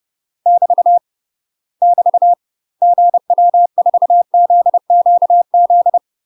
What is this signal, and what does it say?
Morse code, heard as a keyed tone, about 30 wpm, 700 Hz. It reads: <BT> <BT> GW4ZQZ
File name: UK MASTER QSOs TRAINING / 30WPM / UK MASTER QSOs 0018